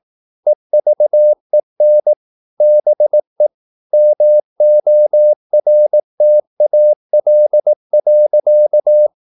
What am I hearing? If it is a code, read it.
Morse code, heard as a keyed tone, about 18 wpm, 600 Hz. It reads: EVEN BE MORTAL.
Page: Romanen Never af Ken Follet på engelsk.